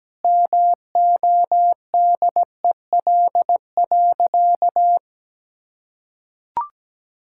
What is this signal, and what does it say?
Morse code, heard as a keyed tone, about 17 wpm, 700 Hz. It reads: MODEL.
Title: Sentences from Top 1000 Words - Review Part 5 - 17wpm